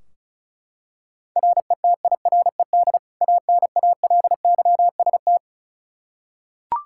Morse code, heard as a keyed tone, about 36 wpm, 700 Hz. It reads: RETIRED ANALYST
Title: K1USN SST QSO Practice 35wpm